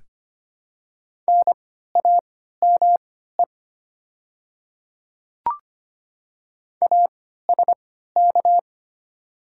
Morse code, heard as a keyed tone, about 25 wpm, spaced as for 14 wpm, 700 Hz. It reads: NAME ASK
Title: 3x Character Spacing - Top 200 Words - 25wpm